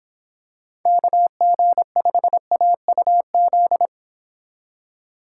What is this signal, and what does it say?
Morse code, heard as a keyed tone, about 26 wpm, 700 Hz. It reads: KG5AUZ